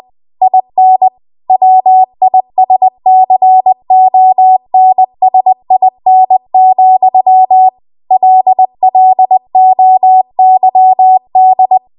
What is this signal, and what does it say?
Morse code, heard as a keyed tone, about 20 wpm, 750 Hz. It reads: IN WISCONSIN, LLOYD
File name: Morse / 20WPM